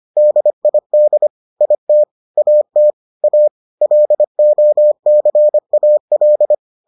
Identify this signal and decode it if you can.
Morse code, heard as a keyed tone, about 25 wpm, 600 Hz. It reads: DID IT AT A LOCAL